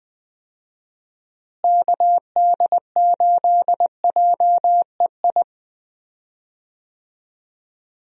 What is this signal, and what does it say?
Morse code, heard as a keyed tone, about 20 wpm, 700 Hz. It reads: KD8JEI
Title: USA Call Signs - 6 Characters - Repeated 3x - 20wpm